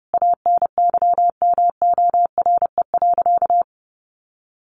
Morse code, heard as a keyed tone, about 28 wpm, 700 Hz. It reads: ANYMORE.
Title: Additional Easy Phrases 30wpm